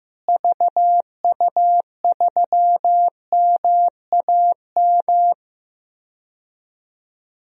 Morse code, heard as a keyed tone, about 15 wpm, 700 Hz. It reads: VU3MAM
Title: DXCC-100 Call Signs - 6 Characters - Repeated 2x - 15wpm